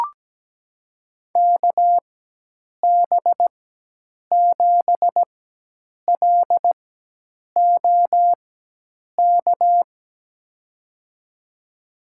Morse code, heard as a keyed tone, about 17 wpm, spaced as for 8 wpm, 700 Hz. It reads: KB7LOK
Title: USA Call Signs - 6 Characters - 4x Character Spacing - 17wpm